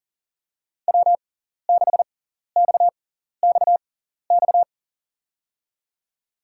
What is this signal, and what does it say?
Morse code, heard as a keyed tone, about 40 wpm, spaced as for 14 wpm, 700 Hz. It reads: W6XXX
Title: USA Call Signs - 6x Character Spacing - 40wpm